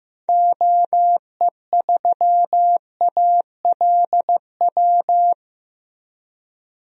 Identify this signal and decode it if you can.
Morse code, heard as a keyed tone, about 15 wpm, 700 Hz. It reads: OE3ALW
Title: DXCC-100 Call Signs - 6 Characters - 15wpm